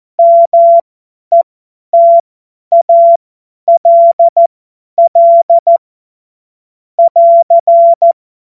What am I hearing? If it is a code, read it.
Morse code, heard as a keyed tone, about 14 wpm, spaced as for 10 wpm, 680 Hz. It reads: METALL <AR>
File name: Morsekurs / Witze / Witz 02. Dezember 2025 50 F70